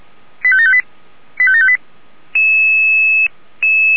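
Five-tone selective call (ZVEI), heard as quick runs of five tones, tones of about 70 ms. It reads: 85758 85758
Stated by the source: ZVEI five-tone call received by SDR, from the feuerwehr-orlamuende.org fire brigade alerting audio archive.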